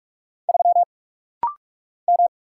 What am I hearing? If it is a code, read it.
Morse code, heard as a keyed tone, about 44 wpm, 700 Hz. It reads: 3 M
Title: Single Letter-Number Alt - Rapid-Fire 45wpm